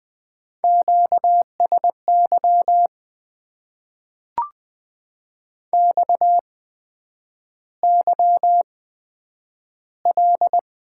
Morse code, heard as a keyed tone, about 20 wpm, spaced as for 9 wpm, 700 Hz. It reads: QSY XYL
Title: Top 90 QSO Elements - 8x Character Spacing - 20wpm